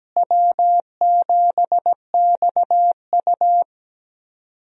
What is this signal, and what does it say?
Morse code, heard as a keyed tone, about 17 wpm, 700 Hz. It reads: W7XU
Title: USA Call Signs - 4 Characters - Repeated 2x - 17wpm